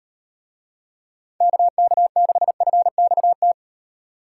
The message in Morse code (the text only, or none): KK6FXT